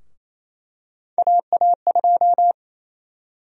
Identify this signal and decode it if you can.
Morse code, heard as a keyed tone, about 28 wpm, 700 Hz. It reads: AA2
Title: USA Call Sign Prefixes - Repeated 2x - 28wpm